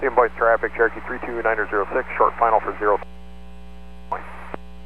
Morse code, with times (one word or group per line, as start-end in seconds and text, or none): none